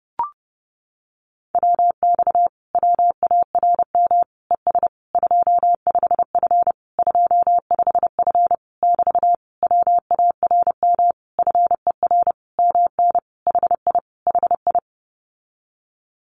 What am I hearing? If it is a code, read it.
Morse code, heard as a keyed tone, about 30 wpm, 700 Hz. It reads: WX WARM ES 25F 25F <BT> WARM FER MN HI HI
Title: Common QSO Phrases Encore 30wpm